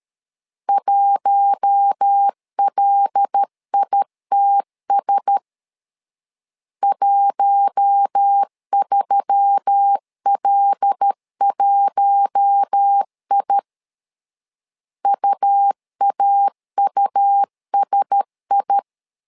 Morse code, heard as a keyed tone, about 13 wpm, 800 Hz. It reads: 1LITS 13L1I UAUSI
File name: LEZIONI / 5 LBVC / 5 L 12wpm 1913